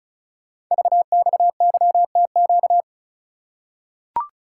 Morse code, heard as a keyed tone, about 36 wpm, 690 Hz. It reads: VXYTQ
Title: Code Groups - 5 Letter:Numbers - 35wpm